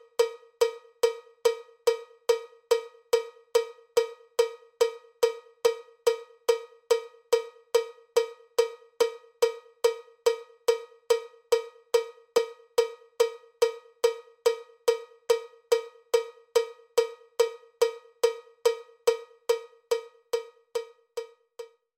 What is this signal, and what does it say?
There is no code.